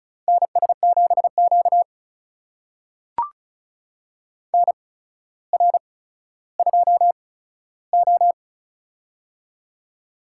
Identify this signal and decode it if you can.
Morse code, heard as a keyed tone, about 36 wpm, spaced as for 15 wpm, 700 Hz. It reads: NS7Q NR2O
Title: USA Call Signs - 4 Characters - 8x Character Spacing - 35wpm